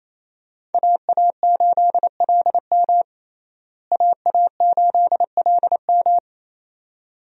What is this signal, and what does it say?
Morse code, heard as a keyed tone, about 28 wpm, 700 Hz. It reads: AA8LM AA8LM
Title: USA Call Signs - 5 Characters - Repeated 2x - 28wpm